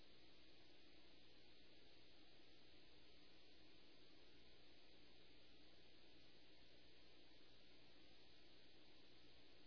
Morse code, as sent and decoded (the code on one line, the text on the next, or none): none